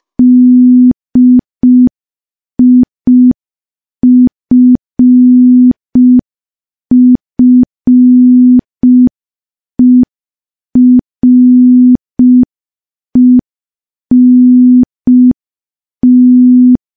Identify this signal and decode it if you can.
Morse code, heard as a keyed tone, about 5 wpm, 250 Hz. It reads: DIFFERENT
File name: morse / morse-training-05WPM